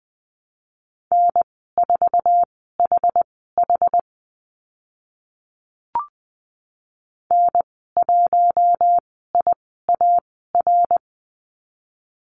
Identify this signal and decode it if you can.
Morse code, heard as a keyed tone, about 20 wpm, spaced as for 15 wpm, 700 Hz. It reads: N4HH N1IAR
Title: 069 Lesson 12g - Letter d - Call Signs 20wpm